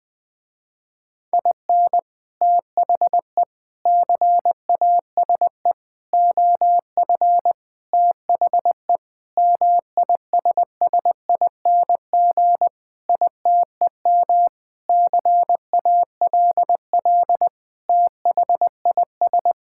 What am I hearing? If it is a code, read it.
Morse code, heard as a keyed tone, about 21 wpm, 700 Hz. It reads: IN THE CASE OF THE MISSING ITEM CALL THIS
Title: Sentences with 5-Word Phrases - Spoken Beginning - 20wpm